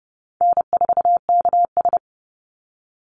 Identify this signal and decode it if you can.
Morse code, heard as a keyed tone, about 30 wpm, 700 Hz. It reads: N4KS
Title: Easy Call Signs Repeated 3x - 30wpm